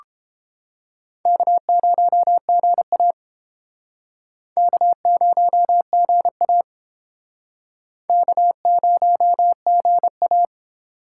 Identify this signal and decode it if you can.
Morse code, heard as a keyed tone, about 30 wpm, 700 Hz. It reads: K0GA K0GA K0GA